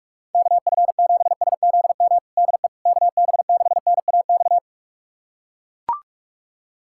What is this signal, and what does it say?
Morse code, heard as a keyed tone, about 43 wpm, 700 Hz. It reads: KF7SZM DE KB6NAX
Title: USA Callsign DE Callsign - 6 Characters - 45wpm